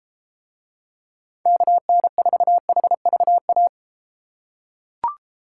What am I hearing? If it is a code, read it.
Morse code, heard as a keyed tone, about 32 wpm, 700 Hz. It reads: KN4HVA